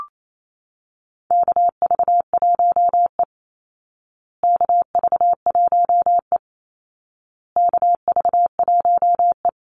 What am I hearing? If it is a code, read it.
Morse code, heard as a keyed tone, about 28 wpm, 700 Hz. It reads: KV1E KV1E KV1E
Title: Call Signs Repeated 2x - 28wpm